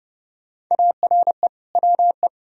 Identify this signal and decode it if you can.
Morse code, heard as a keyed tone, about 32 wpm, 700 Hz. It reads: ARE WE
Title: Sentences from Top 1000 Words - Review Part 4 - 30wpm